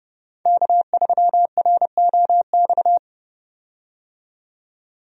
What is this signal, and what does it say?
Morse code, heard as a keyed tone, about 32 wpm, 700 Hz. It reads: K3ROX
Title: USA Call Signs - 5 Characters - 30wpm